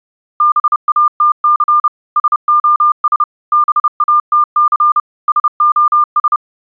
Morse code, heard as a keyed tone, about 30 wpm, 1200 Hz. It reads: BATC SOS BATC SOS
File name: CW battery LOW 30